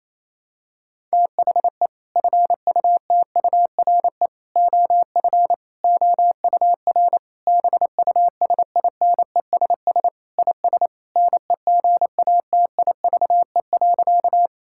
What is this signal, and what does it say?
Morse code, heard as a keyed tone, about 28 wpm, 700 Hz. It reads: THE FUTURE OF OUR BUSINESS IS NEGATIVE.
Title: Sentences from Top 1000 Words - Review Part 6 - 28wpm